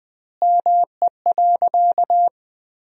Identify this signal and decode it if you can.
Morse code, heard as a keyed tone, about 20 wpm, 700 Hz. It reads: ME.
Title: 2x Word-Spacing - Sentences from Top 200 Words - 20wpm